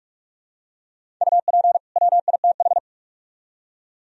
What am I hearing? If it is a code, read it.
Morse code, heard as a keyed tone, about 41 wpm, 700 Hz. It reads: UP WITH